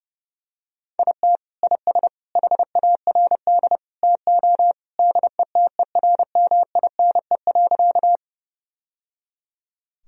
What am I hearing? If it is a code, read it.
Morse code, heard as a keyed tone, about 30 wpm, 700 Hz. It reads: IT IS HARD TO DETERMINE.
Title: Sentences from Top 1000 Words - Review Part 4 - 30wpm